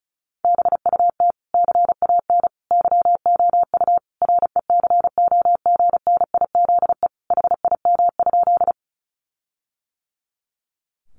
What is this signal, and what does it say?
Morse code, heard as a keyed tone, about 34 wpm, 700 Hz. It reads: BUT CAN YOU RECOGNIZE HIM?
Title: Sentences from Top 1400 Words 35wpm